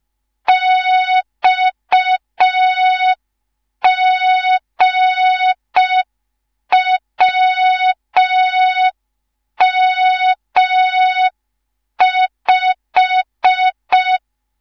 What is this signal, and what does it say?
Morse code, heard as a keyed tone, about 5 wpm, 3700 Hz. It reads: XGWM5
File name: Morze-HA2MN / hang / 19 GYAKORLAT